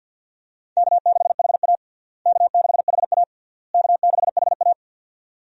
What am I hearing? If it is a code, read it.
Morse code, heard as a keyed tone, about 52 wpm, 700 Hz. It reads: K6HA K6HA K6HA
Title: USA Call Signs - 4 Characters - Repeated 3x - 50wpm